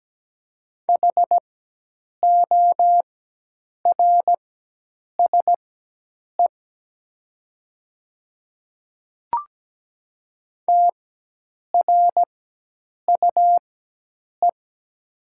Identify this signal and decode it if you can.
Morse code, heard as a keyed tone, about 17 wpm, spaced as for 8 wpm, 700 Hz. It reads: HORSE TRUE
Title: 4x Character Spacing - Top 500 Words - 17wpm